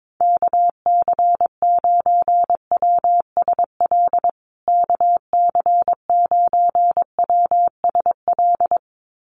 Morse code, heard as a keyed tone, about 22 wpm, 700 Hz. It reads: KC9WSL KC9WSL
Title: USA Call Signs Repeated 2x 22wpm